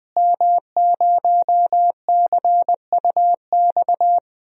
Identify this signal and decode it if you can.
Morse code, heard as a keyed tone, about 20 wpm, 700 Hz. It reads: M0CUX
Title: DXCC-100 Call Signs - 5 Characters - Repeated 3x - 20wpm